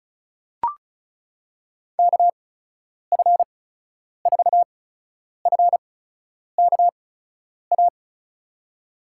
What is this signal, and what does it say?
Morse code, heard as a keyed tone, about 36 wpm, spaced as for 10 wpm, 700 Hz. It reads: KF4FKA
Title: USA Call Signs - 8x Character Spacing - 35wpm